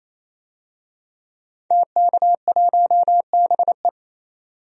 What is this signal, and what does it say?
Morse code, heard as a keyed tone, about 28 wpm, 700 Hz. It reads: TK1BE
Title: DXCC-100 Call Signs - 5 Characters - 28wpm